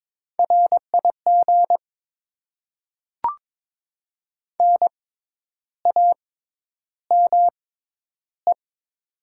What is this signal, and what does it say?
Morse code, heard as a keyed tone, about 21 wpm, spaced as for 10 wpm, 690 Hz. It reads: RIG NAME
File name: Top 20 QSO Elements - 6x Character Spacing - 22wpm